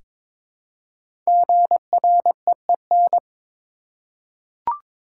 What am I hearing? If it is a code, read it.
Morse code, heard as a keyed tone, about 22 wpm, 700 Hz. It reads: GREEN
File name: 5 Letter Words - First 3 Letters Given - 22wpm